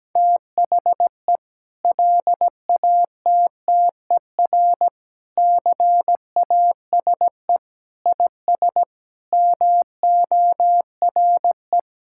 Morse code, heard as a keyed tone, about 17 wpm, 700 Hz. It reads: THE LATTER CASE IS MORE
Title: Sentences from Top 1600 Words 17wpm